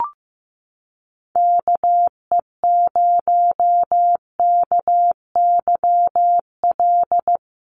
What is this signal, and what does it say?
Morse code, heard as a keyed tone, about 15 wpm, 700 Hz. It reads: KE0KYL
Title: USA Call Signs Repeated 3x 15wpm